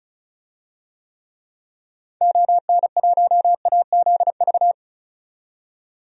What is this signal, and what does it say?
Morse code, heard as a keyed tone, about 36 wpm, 700 Hz. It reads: ON1AZV
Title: DXCC-100 Call Signs - 6 Characters - 35wpm